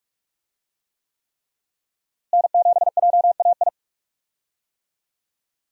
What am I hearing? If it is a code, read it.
Morse code, heard as a keyed tone, about 41 wpm, 700 Hz. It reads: N7JAI
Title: USA Call Signs - 5 Characters - Repeated 3x - 45wpm